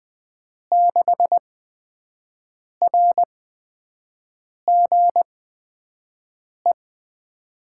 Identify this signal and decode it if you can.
Morse code, heard as a keyed tone, about 20 wpm, 700 Hz. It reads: 6 R G E
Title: USA Call Signs - 5 Characters - 8x Character Spacing - 20wpm